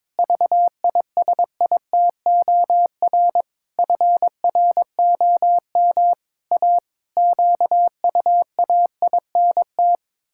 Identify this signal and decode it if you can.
Morse code, heard as a keyed tone, about 22 wpm, 700 Hz. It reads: VISITOR FROM A QUAINT
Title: Pangram Challenge v2 - 22wpm